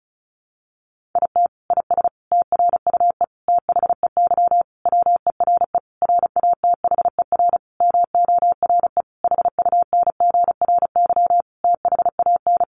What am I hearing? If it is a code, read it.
Morse code, heard as a keyed tone, about 34 wpm, 700 Hz. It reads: IT IS TRUE THEY WERE RATHER MORE HUNGRY THAN
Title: Sentences from Top 1400 Words 35wpm